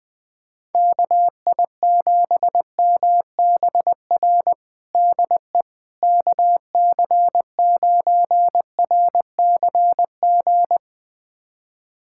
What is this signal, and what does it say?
Morse code, heard as a keyed tone, about 20 wpm, 700 Hz. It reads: KI7MBR DE KC9RCG
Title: USA Callsign DE Callsign - 6 Characters - 20wpm